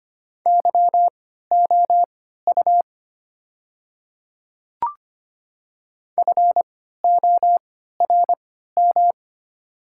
Morse code, heard as a keyed tone, about 25 wpm, spaced as for 14 wpm, 700 Hz. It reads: YOU FORM
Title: 3x Character Spacing - Top 200 Words - 25wpm